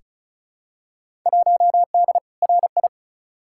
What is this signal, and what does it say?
Morse code, heard as a keyed tone, about 36 wpm, 700 Hz. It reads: 1D RI
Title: Field Day Exchanges - 35wpm